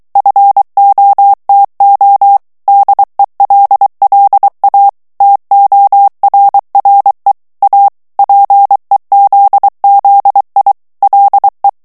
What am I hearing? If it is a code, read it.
Morse code, heard as a keyed tone, about 23 wpm, 800 Hz. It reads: FOTO DELLA TORRE A PEZZI LE
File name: morse mp3 / ITALIEN texte clair 1 vitesse 22 mots minute